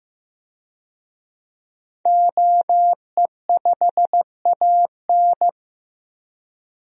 Morse code, heard as a keyed tone, about 15 wpm, 700 Hz. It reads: OE5AN
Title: DXCC-100 Call Signs - 5 Characters - 15wpm